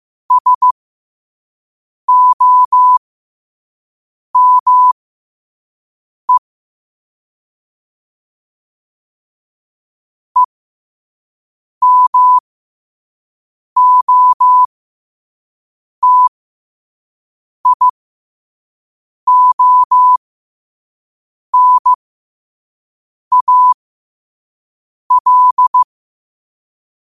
Morse code, heard as a keyed tone, about 15 wpm, spaced as for 5 wpm, 1000 Hz. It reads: SOME EMOTIONAL
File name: qotdfiles / 5wpm